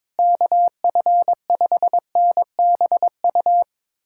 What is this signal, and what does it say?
Morse code, heard as a keyed tone, about 22 wpm, 700 Hz. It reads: KF5NBU
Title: USA Callsign DE Callsign - 6 Characters - 22wpm